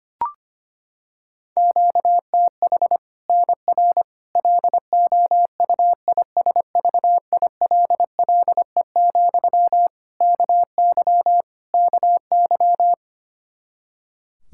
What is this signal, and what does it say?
Morse code, heard as a keyed tone, about 25 wpm, 700 Hz. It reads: QTH NR LOUISVILLE, KY KY
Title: K1USN SST QSO Practice 25wpm